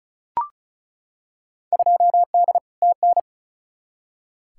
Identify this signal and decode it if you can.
Morse code, heard as a keyed tone, about 36 wpm, 700 Hz. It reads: 2D TN
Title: Field Day Exchanges Encore - 35wpm